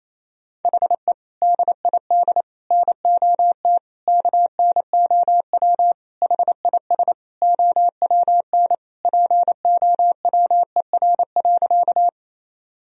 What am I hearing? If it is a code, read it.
Morse code, heard as a keyed tone, about 27 wpm, 700 Hz. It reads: HE DID NOT KNOW HIS OWN POWER.
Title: Sentences from Top 1000 Words - Review Part 2 - 28wpm